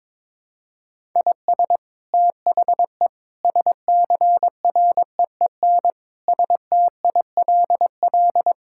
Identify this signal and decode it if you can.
Morse code, heard as a keyed tone, about 22 wpm, 700 Hz. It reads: IS THE SCREEN STILL
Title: Sentences from Top 1000 Words - Review Part 6 - 22wpm